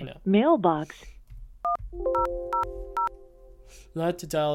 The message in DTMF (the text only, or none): 1000